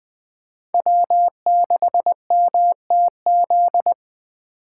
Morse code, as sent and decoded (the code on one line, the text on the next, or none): .-- -.... -- - --..
W6MTZ